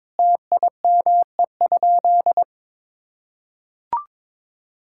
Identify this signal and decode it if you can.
Morse code, heard as a keyed tone, about 22 wpm, 700 Hz. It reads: TIME?
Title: Sentences from Top 1000 Words - Review Part 5 - 22wpm